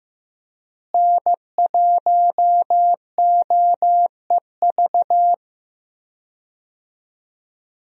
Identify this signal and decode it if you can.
Morse code, heard as a keyed tone, about 15 wpm, 700 Hz. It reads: N1OEV
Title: USA Call Signs - 5 Characters - Repeated 3x - 15wpm